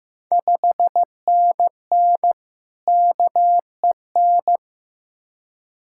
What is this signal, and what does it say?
Morse code, heard as a keyed tone, about 15 wpm, 700 Hz. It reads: 5NN KEN
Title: Ontario QSO Party Exchange 15wpm